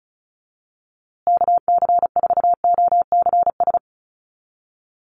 Morse code, heard as a keyed tone, about 34 wpm, 700 Hz. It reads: KC4OCS
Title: USA Call Signs Repeated 2x 35wpm